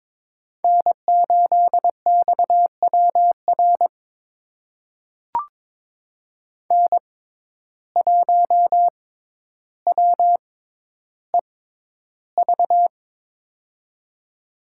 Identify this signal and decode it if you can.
Morse code, heard as a keyed tone, about 22 wpm, spaced as for 11 wpm, 700 Hz. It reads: N8XWR N1WEV
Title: USA Call Signs - 5 Characters - 6x Character Spacing - 22wpm